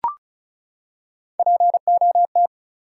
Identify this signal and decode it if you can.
Morse code, heard as a keyed tone, about 36 wpm, 700 Hz. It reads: POT
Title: Three Letter Words - 35wpm